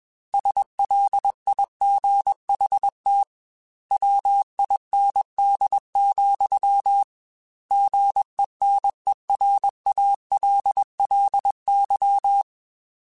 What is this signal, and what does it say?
Morse code, heard as a keyed tone, about 21 wpm, 800 Hz. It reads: SLIGHT WIND, GENERALLY